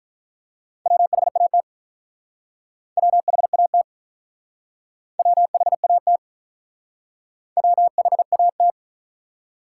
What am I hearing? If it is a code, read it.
Morse code, heard as a keyed tone, about 43 wpm, 700 Hz. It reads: WHAT WHAT WHAT WHAT
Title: Speed-Racing Top 180 QSO Elements 35wpm